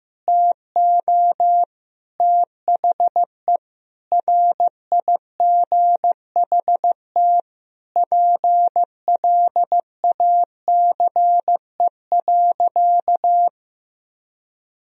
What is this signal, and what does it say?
Morse code, heard as a keyed tone, about 15 wpm, 700 Hz. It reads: TO THE RIGHT PLACE.
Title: Sentences from Top 1000 Words - Review Part 3 - 15wpm